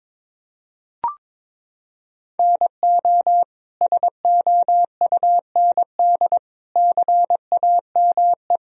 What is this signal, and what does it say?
Morse code, heard as a keyed tone, about 22 wpm, 700 Hz. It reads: NO SOUND CAME
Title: Sentences from Top 1000 Words - Review Part 4 - 22wpm